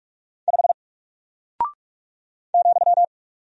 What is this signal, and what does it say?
Morse code, heard as a keyed tone, about 44 wpm, 700 Hz. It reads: 5 ,
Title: Single Character (Letter:Number:Punctuation:Prosign) Alt 45wpm